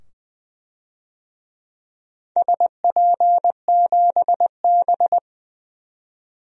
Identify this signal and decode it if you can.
Morse code, heard as a keyed tone, about 20 wpm, 700 Hz. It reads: SP7B
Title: DXCC-100 Call Signs - 4 Characters - 20wpm